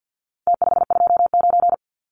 Morse code, heard as a keyed tone, about 50 wpm, 700 Hz. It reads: T529
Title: Numbers - Contest Style - 4-Digit 50wpm